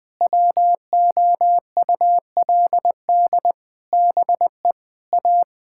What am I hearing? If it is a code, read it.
Morse code, heard as a keyed tone, about 21 wpm, 700 Hz. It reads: WOULD BE A
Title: Sets of 8 Words - 20wpm